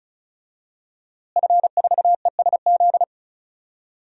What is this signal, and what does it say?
Morse code, heard as a keyed tone, about 36 wpm, 700 Hz. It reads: F4ESZ
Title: DXCC-100 Call Signs - 5 Characters - Repeated 2x - 35wpm